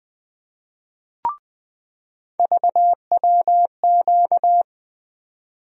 Morse code, heard as a keyed tone, about 20 wpm, 700 Hz. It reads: VWQ
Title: USA Call Sign Postfixes - 3 Letters 20wpm